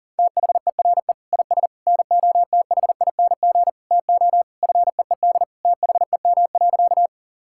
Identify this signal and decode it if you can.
Morse code, heard as a keyed tone, about 40 wpm, 700 Hz. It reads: THERE IS NOTHING TO FEED THEM.
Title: Sentences from Top 1600 Words 40wpm